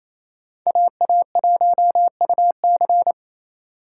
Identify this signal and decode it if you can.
Morse code, heard as a keyed tone, about 28 wpm, 700 Hz. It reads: AA1UC
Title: K1USN SST QSO Practice - 28wpm